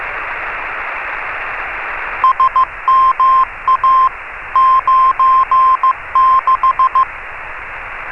Morse code, heard as a keyed tone, about 15 wpm, 1050 Hz. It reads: SMA 96
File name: RS30-CW-Beacon